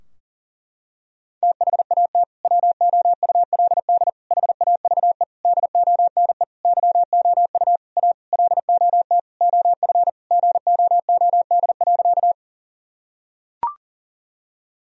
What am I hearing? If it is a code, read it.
Morse code, heard as a keyed tone, about 40 wpm, 700 Hz. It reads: THAT WOULD HAVE DONE YOU A LOT OF GOOD.
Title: Sentences from Top 1000 Words - Review Part 3 - 40wpm